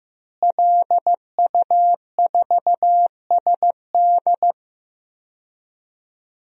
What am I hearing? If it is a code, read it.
Morse code, heard as a keyed tone, about 15 wpm, 700 Hz. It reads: LU4SD
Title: DXCC-100 Call Signs - 5 Characters - Repeated 2x - 15wpm